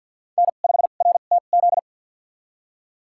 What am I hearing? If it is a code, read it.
Morse code, heard as a keyed tone, about 51 wpm, spaced as for 36 wpm, 700 Hz. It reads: N5RTZ
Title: USA Call Signs - 5 Characters - 2x Character Spacing - 50wpm